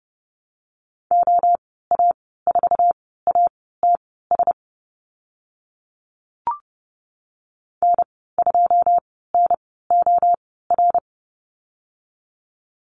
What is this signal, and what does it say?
Morse code, heard as a keyed tone, about 30 wpm, spaced as for 17 wpm, 700 Hz. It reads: OA4ATS N2NOR
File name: Lesson 15g - Number 5 - Call Signs 30wpm